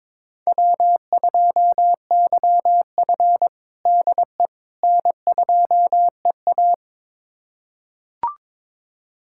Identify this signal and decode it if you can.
Morse code, heard as a keyed tone, about 21 wpm, 700 Hz. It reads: W2YF DE N2EA
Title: USA Callsign DE Callsign - 4 Characters - 22wpm